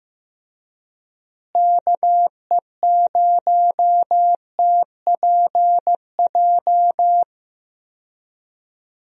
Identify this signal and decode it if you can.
Morse code, heard as a keyed tone, about 16 wpm, 700 Hz. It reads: KE0TPJ
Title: USA Call Signs - 6 Characters - Repeated 2x - 15wpm